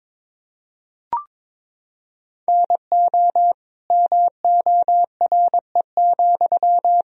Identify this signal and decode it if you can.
Morse code, heard as a keyed tone, about 22 wpm, 700 Hz. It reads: NO MORE,
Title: Sentences from Top 1000 Words - Review Part 2 - 22wpm